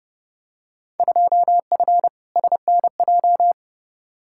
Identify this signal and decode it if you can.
Morse code, heard as a keyed tone, about 30 wpm, 700 Hz. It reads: 2F SNJ
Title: Field Day Exchanges Encore - 30wpm